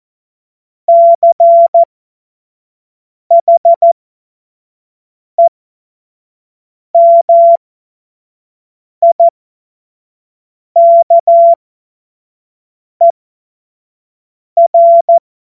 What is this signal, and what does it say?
Morse code, heard as a keyed tone, about 14 wpm, spaced as for 5 wpm, 680 Hz. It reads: CHEMIKER